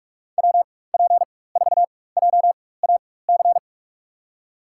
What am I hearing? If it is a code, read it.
Morse code, heard as a keyed tone, about 46 wpm, spaced as for 21 wpm, 700 Hz. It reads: WP4JAC
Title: USA Call Signs - 6 Characters - 4x Character Spacing - 45wpm